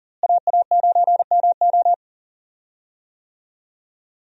Morse code, heard as a keyed tone, about 40 wpm, 700 Hz. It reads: AA9MO
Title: USA Call Signs - 5 Characters - Repeated 2x - 40wpm